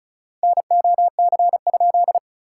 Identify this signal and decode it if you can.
Morse code, heard as a keyed tone, about 34 wpm, 690 Hz. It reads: NOC?
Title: SOTA QSO Practice 35wpm